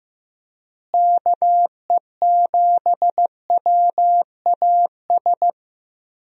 Morse code, heard as a keyed tone, about 15 wpm, 700 Hz. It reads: KE7WAS
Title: USA Call Signs - 6 Characters - Repeated 3x - 15wpm